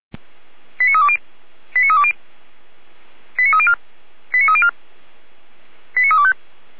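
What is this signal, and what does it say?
Five-tone selective call (ZVEI), heard as quick runs of five tones, tones of about 80 ms. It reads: 89320 89320 89304 89304 89325
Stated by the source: ZVEI five-tone call received with an SDR for the Feuerwehr Orlamünde fire brigade.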